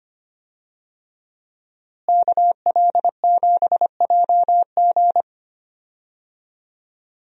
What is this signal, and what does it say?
Morse code, heard as a keyed tone, about 25 wpm, 700 Hz. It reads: KL7JG
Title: DXCC-100 Call Signs - 5 Characters - Repeated 2x - 25wpm